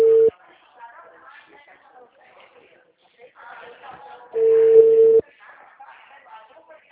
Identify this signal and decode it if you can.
Ringback tone (China, 450 Hz), with speech.
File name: out- -2011- - - .57263